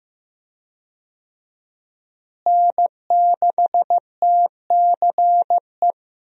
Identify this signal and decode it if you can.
Morse code, heard as a keyed tone, about 15 wpm, 700 Hz. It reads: N6TCE
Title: USA Call Signs - 5 Characters - Repeated 3x - 15wpm